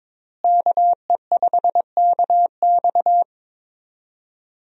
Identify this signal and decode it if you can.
Morse code, heard as a keyed tone, about 21 wpm, 700 Hz. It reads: KE5KX
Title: USA Call Signs - 5 Characters - Repeated 3x - 22wpm